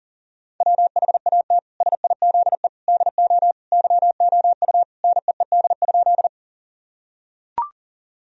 Morse code, heard as a keyed tone, about 40 wpm, 700 Hz. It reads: WHAT SIZE DO YOU NEED?
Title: Sentences from Top 1000 Words - Review Part 2 - 40wpm